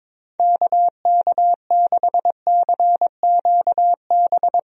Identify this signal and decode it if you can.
Morse code, heard as a keyed tone, about 22 wpm, 700 Hz. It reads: KK6CQB
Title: USA Call Signs - 6 Characters - Repeated 2x - 22wpm